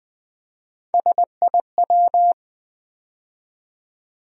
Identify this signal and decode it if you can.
Morse code, heard as a keyed tone, about 20 wpm, 700 Hz. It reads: SIW
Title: USA Call Sign Postfixes - 3 Letters 20wpm